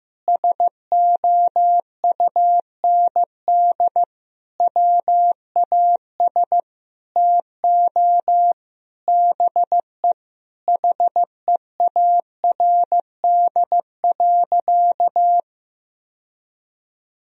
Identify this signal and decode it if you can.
Morse code, heard as a keyed tone, about 15 wpm, 700 Hz. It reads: SOUND WAS TO BE HEARD.
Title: Sentences from Top 1000 Words - Review Part 1 - 15wpm